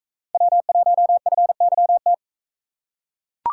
Morse code, heard as a keyed tone, about 41 wpm, 700 Hz. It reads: W1FYT